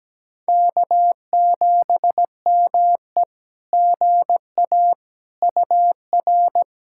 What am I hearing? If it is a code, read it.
Morse code, heard as a keyed tone, about 17 wpm, 700 Hz. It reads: K7ME GA UR
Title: SOTA QSO Practice 17wpm